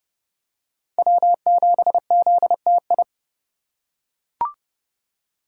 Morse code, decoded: W7ZTI

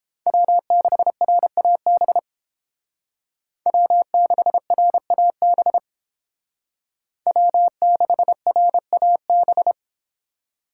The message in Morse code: W6RAB W6RAB W6RAB

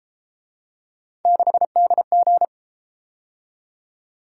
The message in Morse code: 6DG

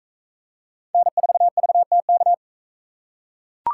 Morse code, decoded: N4VTK

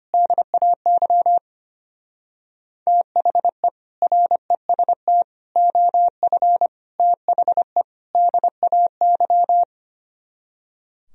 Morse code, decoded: DAY THE REST OF THE DAY

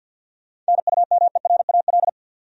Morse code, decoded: NUMERAL

